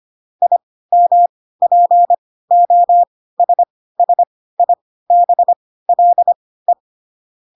Morse code, heard as a keyed tone, about 25 wpm, spaced as for 16 wpm, 700 Hz. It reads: IMPOSSIBLE
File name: Pinocchio / Pinocchio ch5 10 25 ik1hge